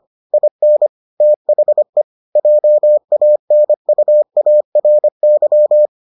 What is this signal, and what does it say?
Morse code, heard as a keyed tone, about 25 wpm, 600 Hz. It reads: IN THE JANUARY